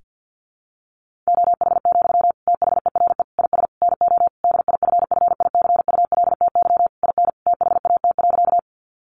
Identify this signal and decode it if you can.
Morse code, heard as a keyed tone, about 50 wpm, 700 Hz. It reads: OH, THERE IS NO DIFFICULTY IN THAT.
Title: Sentences from Top 1500 Words 50wpm